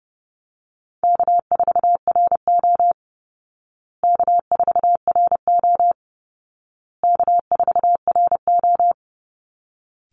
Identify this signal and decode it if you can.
Morse code, heard as a keyed tone, about 30 wpm, 700 Hz. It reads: K4RO K4RO K4RO
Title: Easy Call Signs Repeated 3x - 30wpm